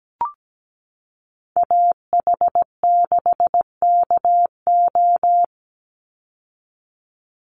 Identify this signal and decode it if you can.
Morse code, heard as a keyed tone, about 17 wpm, 700 Hz. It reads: AH6KO
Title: Call Signs Repeated 3x - 17wpm